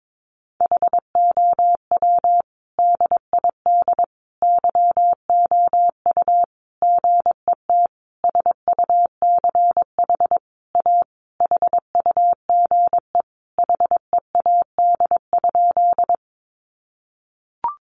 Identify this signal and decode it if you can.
Morse code, heard as a keyed tone, about 22 wpm, 700 Hz. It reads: HOW DID YOU GET SUCH A HUGE HEAD?
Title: Sentences from Top 1100 Words 22wpm